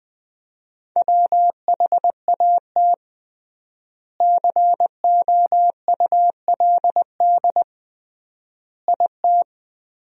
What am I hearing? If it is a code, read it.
Morse code, heard as a keyed tone, about 20 wpm, 700 Hz. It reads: WHAT COULD IT
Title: 3x Word-Spacing - Sentences from Top 100 Words - 20wpm